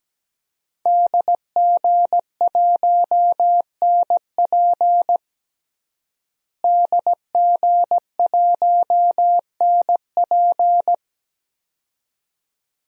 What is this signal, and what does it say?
Morse code, heard as a keyed tone, about 17 wpm, 700 Hz. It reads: DG1NP DG1NP
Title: DXCC-100 Call Signs - 5 Characters - Repeated 2x - 17wpm